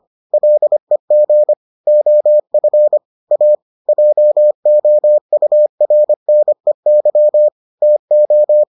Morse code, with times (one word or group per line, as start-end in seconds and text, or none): 0.3-1.5 LEG
1.9-3.0 OF
3.3-3.5 A
3.9-7.5 JOURNEY
7.8-8.6 TO